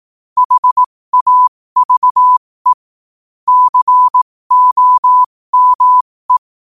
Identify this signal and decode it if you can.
Morse code, heard as a keyed tone, about 18 wpm, spaced as for 15 wpm, 1000 Hz. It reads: HAVE COME
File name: qotdfiles / 15wpm